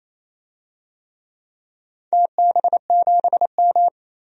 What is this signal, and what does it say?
Morse code, heard as a keyed tone, about 27 wpm, 700 Hz. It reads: TB7M
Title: DXCC-100 Call Signs - 4 Characters - 28wpm